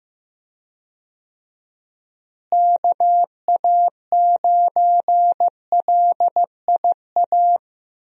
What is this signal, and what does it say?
Morse code, heard as a keyed tone, about 15 wpm, 700 Hz. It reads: KA9LIA
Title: USA Call Signs - 6 Characters - Repeated 2x - 15wpm